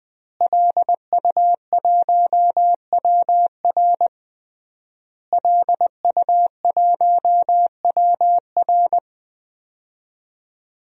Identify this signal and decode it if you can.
Morse code, heard as a keyed tone, about 20 wpm, 700 Hz. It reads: LU1WR LU1WR
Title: DXCC-100 Call Signs - 5 Characters - Repeated 3x - 20wpm